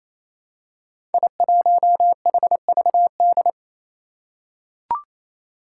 Morse code, heard as a keyed tone, about 27 wpm, 700 Hz. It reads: I1HVD
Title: Five Challenging Letter-Number Combinations 28wpm